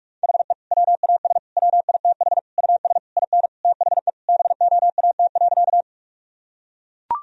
Morse code, heard as a keyed tone, about 41 wpm, 700 Hz. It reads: HE WAS WITH US IN THE BOAT.
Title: Sentences from Top 1000 Words - Review Part 3 - 45wpm